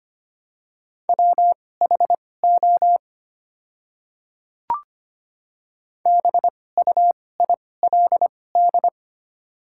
Morse code, heard as a keyed tone, about 25 wpm, spaced as for 18 wpm, 700 Hz. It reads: WHO BUILD
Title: 2x Character Spacing - Top 200 Words - 25wpm